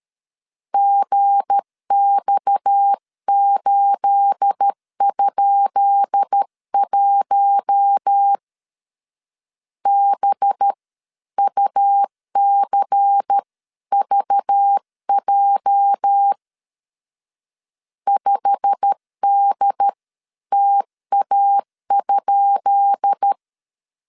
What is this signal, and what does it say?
Morse code, heard as a keyed tone, about 13 wpm, spaced as for 11 wpm, 800 Hz. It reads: GX8?1 BUCVJ 5DTA?
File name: LEZIONI / 10 SIGNS / 10B INT 12wpm 2414